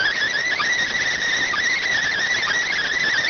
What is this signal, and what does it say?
Emergency Alert System (SAME) tones, heard as harsh buzzing bursts. It is heard from 0.4 to 3.1 s.